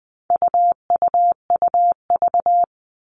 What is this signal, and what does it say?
Morse code, heard as a keyed tone, about 20 wpm, 700 Hz. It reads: UUUV
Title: All Cut Numbers - 4-Digit 20wpm